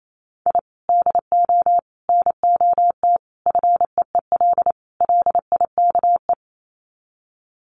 Morse code, heard as a keyed tone, about 28 wpm, 700 Hz. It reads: I DO NOT FEEL LIKE
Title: Sets of 5 Words Encore 28wpm